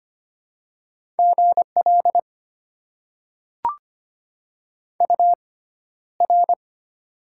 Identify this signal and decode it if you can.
Morse code, heard as a keyed tone, about 25 wpm, spaced as for 13 wpm, 700 Hz. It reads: GL UR
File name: Top 90 QSO Elements - 6x Character Spacing - 25wpm